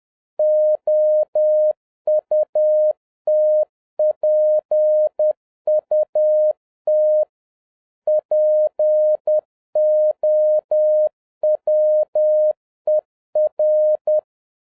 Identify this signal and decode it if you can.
Morse code, heard as a keyed tone, about 10 wpm, 610 Hz. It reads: OUTPUT POWER
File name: 2011-07-mp3 / n41 EXCITING QRP CW OPERATION 10wpm